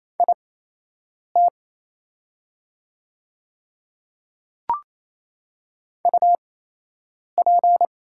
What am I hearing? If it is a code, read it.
Morse code, heard as a keyed tone, about 28 wpm, spaced as for 8 wpm, 700 Hz. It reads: IT UP
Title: 8x Character Spacing - Top 500 Words - 28wpm